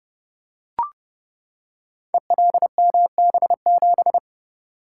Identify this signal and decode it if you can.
Morse code, heard as a keyed tone, about 32 wpm, 700 Hz. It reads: ELMB7
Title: Code Groups - 5 Letter:Numbers - 30wpm